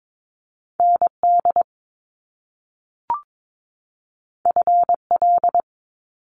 Morse code, heard as a keyed tone, about 22 wpm, 700 Hz. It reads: ND FL
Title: US States 22wpm